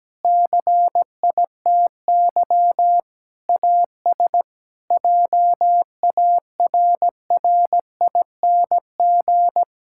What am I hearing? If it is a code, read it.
Morse code, heard as a keyed tone, about 17 wpm, 700 Hz. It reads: CITY AS JARRING
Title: Pangram Challenge v2 - 17wpm